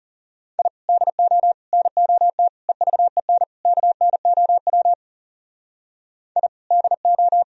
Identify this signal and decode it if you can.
Morse code, heard as a keyed tone, about 37 wpm, 700 Hz. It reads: I DO NOT EVEN KNOW I DO
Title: Speed-Racing - Sets of 5 Words 30wpm